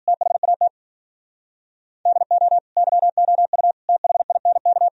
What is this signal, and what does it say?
Morse code, heard as a keyed tone, about 51 wpm, 700 Hz. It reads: THAT DO YOU THINK